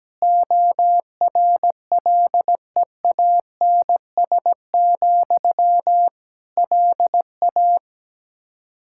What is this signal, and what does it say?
Morse code, heard as a keyed tone, about 17 wpm, 700 Hz. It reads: ORLEANS, LA
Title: K1USN SST QSO Practice 17wpm